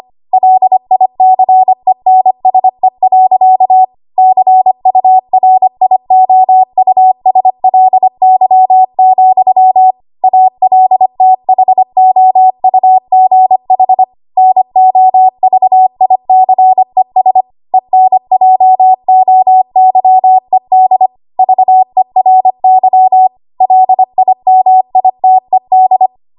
Morse code, heard as a keyed tone, about 25 wpm, 750 Hz. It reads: LICENSE. CURIOUSLY, ALTHOUGH NOVICES ENJOYED VERY LIMITED
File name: Archive / 25 WPM / 25WPM